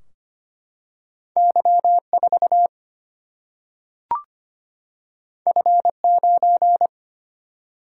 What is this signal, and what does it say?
Morse code, heard as a keyed tone, about 25 wpm, 700 Hz. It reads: Y4 F9
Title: Single-Letter Single-Number 25wpm